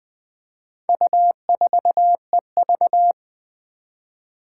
Morse code, heard as a keyed tone, about 21 wpm, 700 Hz. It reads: U4EV
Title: Four Challenging Letter-Number Combinations 20wpm